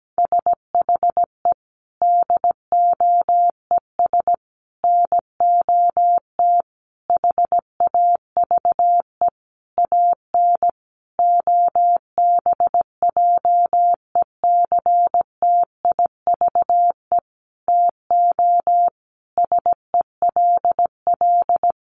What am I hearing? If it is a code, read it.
Morse code, heard as a keyed tone, about 17 wpm, 700 Hz. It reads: SHE DOES NOT HAVE AN OBJECTIVE TO SELL
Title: Sentences from Top 1200 Words 17wpm